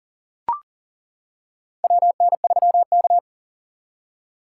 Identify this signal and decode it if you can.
Morse code, heard as a keyed tone, about 40 wpm, 700 Hz. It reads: WN3K
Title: K1USN SST QSO Practice - 40wpm